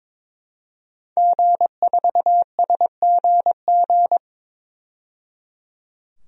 Morse code, heard as a keyed tone, about 22 wpm, 700 Hz. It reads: G4SGG
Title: DXCC-100 Call Signs - 5 Characters - 22wpm